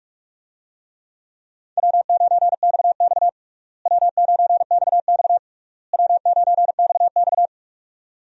Morse code, heard as a keyed tone, about 46 wpm, 700 Hz. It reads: W9XX W9XX W9XX
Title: USA Call Signs - 4 Characters - Repeated 3x - 45wpm